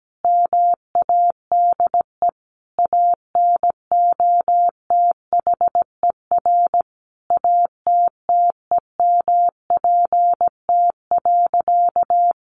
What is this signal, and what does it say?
Morse code, heard as a keyed tone, about 17 wpm, 700 Hz. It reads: MADE ANOTHER ATTEMPT.
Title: Sentences from Top 1000 Words 17wpm